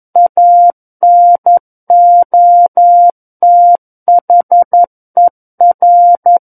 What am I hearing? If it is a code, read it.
Morse code, heard as a keyed tone, about 11 wpm, 700 Hz. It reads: ANOTHER
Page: SKCC - War of the Worlds - 02-Chapter 2 - 11 WPM